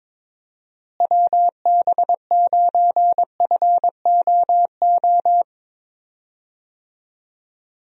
Morse code, heard as a keyed tone, about 22 wpm, 700 Hz. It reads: WB9FOO
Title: USA Call Signs - 6 Characters - Repeated 3x - 22wpm